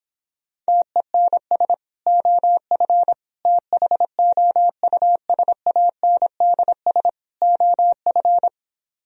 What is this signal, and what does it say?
Morse code, heard as a keyed tone, about 26 wpm, 700 Hz. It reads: TENS OF THOUSANDS OF